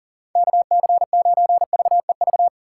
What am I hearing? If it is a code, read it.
Morse code, heard as a keyed tone, about 40 wpm, 700 Hz. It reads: KC9VEV
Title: USA Call Signs - 6 Characters - 6x Character Spacing - 40wpm